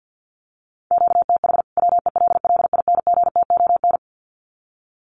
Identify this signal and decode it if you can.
Morse code, heard as a keyed tone, about 50 wpm, 700 Hz. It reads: QTH WELLINGTON
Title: Common QSO Phrases 50wpm